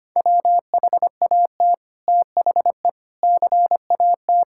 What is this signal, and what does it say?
Morse code, heard as a keyed tone, about 25 wpm, 690 Hz. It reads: WHAT THE CAT
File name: practice / 25wpm / Sentences with 2-Word Phrases - 25wpm